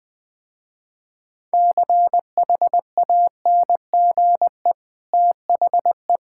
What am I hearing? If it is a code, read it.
Morse code, heard as a keyed tone, about 20 wpm, 700 Hz. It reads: CHANGE THE
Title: Easy Sentences with Hints - 20wpm